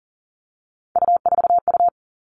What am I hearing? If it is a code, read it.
Morse code, heard as a keyed tone, about 40 wpm, 700 Hz. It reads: U4U
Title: Three Challenging Letter-Number Combinations 40wpm